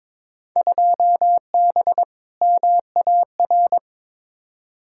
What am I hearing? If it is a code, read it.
Morse code, heard as a keyed tone, about 22 wpm, 700 Hz. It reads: 2B MAR
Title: Field Day Exchanges Encore - 22wpm